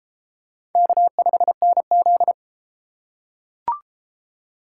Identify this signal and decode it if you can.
Morse code, heard as a keyed tone, about 32 wpm, 700 Hz. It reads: K5NZ